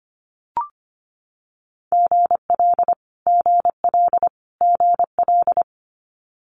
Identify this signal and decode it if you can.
Morse code, heard as a keyed tone, about 25 wpm, 700 Hz. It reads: GL GL GL
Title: Multi-letter Phonemes Repeated 3x 25wpm